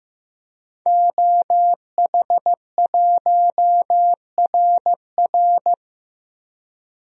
Morse code, heard as a keyed tone, about 15 wpm, 700 Hz. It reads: OH1RR
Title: DXCC-100 Call Signs - 5 Characters - Repeated 3x - 15wpm